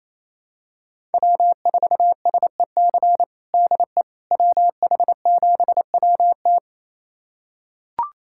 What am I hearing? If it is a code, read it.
Morse code, heard as a keyed tone, about 27 wpm, 700 Hz. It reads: W4SEC DE WH7WT
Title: USA Callsign DE Callsign - 5 Characters - 28wpm